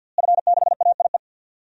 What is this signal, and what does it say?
Morse code, heard as a keyed tone, about 51 wpm, 700 Hz. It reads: F6AIE
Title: DXCC-100 Call Signs - 5 Characters - Repeated 3x - 50wpm